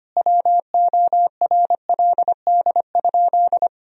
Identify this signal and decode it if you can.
Morse code, heard as a keyed tone, about 25 wpm, 700 Hz. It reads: WORLD?
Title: Sentences from Top 200 Words and Abbreviations 25wpm